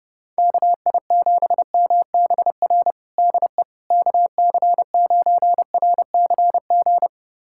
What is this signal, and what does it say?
Morse code, heard as a keyed tone, about 32 wpm, 700 Hz. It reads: KI7MBR DE KC9RCG
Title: USA Callsign DE Callsign - 6 Characters - 30wpm